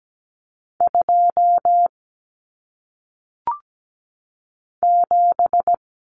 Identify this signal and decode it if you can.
Morse code, heard as a keyed tone, about 17 wpm, 700 Hz. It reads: 2 7
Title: Common Cut Numbers - 1-Digit 17wpm